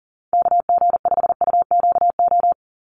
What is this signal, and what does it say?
Morse code, heard as a keyed tone, about 40 wpm, 700 Hz. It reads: KG5UQO
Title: USA Call Signs 40wpm